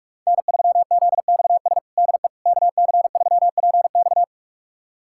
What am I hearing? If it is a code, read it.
Morse code, heard as a keyed tone, about 47 wpm, 700 Hz. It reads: N3ZXS DE KC3PX
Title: USA Callsign DE Callsign - 5 Characters - 45wpm